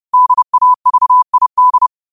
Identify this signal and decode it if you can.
Morse code, heard as a keyed tone, about 30 wpm, 1000 Hz. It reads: DAVID
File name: qotdfiles / 30wpm